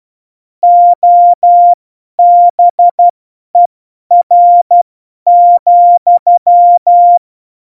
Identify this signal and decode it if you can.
Morse code, heard as a keyed tone, about 12 wpm, spaced as for 10 wpm, 700 Hz. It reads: OBER,
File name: Morsekurs / Witze / Witz 20. November 2018, 50 F70